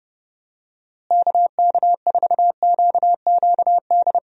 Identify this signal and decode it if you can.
Morse code, heard as a keyed tone, about 31 wpm, 700 Hz. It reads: KK4QQD
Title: USA Call Signs - 6 Characters - Repeated 2x - 30wpm